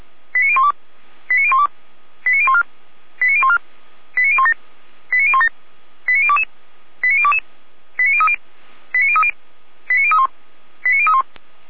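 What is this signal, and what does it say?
Five-tone selective call (ZVEI), heard as quick runs of five tones, tones of about 80 ms. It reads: 89012 89012 89014 89014 89017 89017 89022 89022 89030 89030 89031 89031
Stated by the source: ZVEI five-tone selective call received on an SDR.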